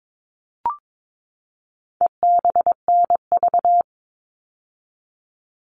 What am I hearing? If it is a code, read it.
Morse code, heard as a keyed tone, about 22 wpm, 700 Hz. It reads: EBNV
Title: All Cut Numbers - 4-Digit 22wpm